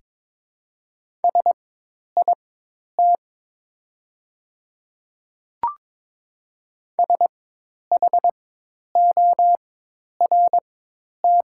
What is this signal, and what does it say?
Morse code, heard as a keyed tone, about 22 wpm, spaced as for 10 wpm, 700 Hz. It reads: SIT SHORT
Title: 4x Character Spacing - Top 500 Words - 22wpm